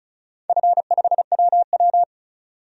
Morse code, heard as a keyed tone, about 36 wpm, 700 Hz. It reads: F5WW
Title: DXCC-100 Call Signs - 4 Characters - 35wpm